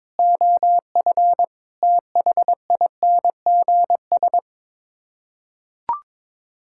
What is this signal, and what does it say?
Morse code, heard as a keyed tone, about 22 wpm, 700 Hz. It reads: OF THINGS
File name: practice / 22wpm / Sets of 6 Words - 22wpm